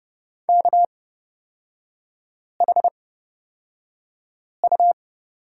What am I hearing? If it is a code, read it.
Morse code, heard as a keyed tone, about 29 wpm, spaced as for 5 wpm, 700 Hz. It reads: KHU
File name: Lesson 37e - Letter x - Three Characters 30wpm